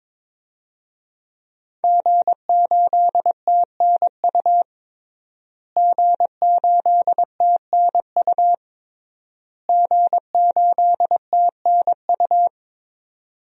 Morse code, heard as a keyed tone, about 22 wpm, 700 Hz. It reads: G8TNU G8TNU G8TNU
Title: DXCC-100 Call Signs - 5 Characters - Repeated 3x - 22wpm